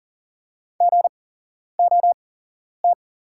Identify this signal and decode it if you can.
Morse code, heard as a keyed tone, about 40 wpm, spaced as for 11 wpm, 690 Hz. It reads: GOT
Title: 8x Character Spacing - Top 500 Words - 40wpm